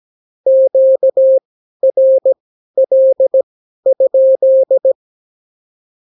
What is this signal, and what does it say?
Morse code, heard as a keyed tone, about 17 wpm, spaced as for 12 wpm, 530 Hz. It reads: QRL?
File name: CW QSO libroCW4 18-8ppm